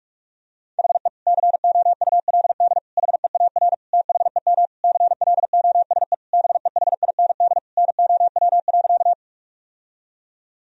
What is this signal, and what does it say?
Morse code, heard as a keyed tone, about 41 wpm, 700 Hz. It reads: HE COULD HEAR THEM CLOSE BEHIND NOW.
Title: Sentences from Top 1000 Words - Review Part 1 - 45wpm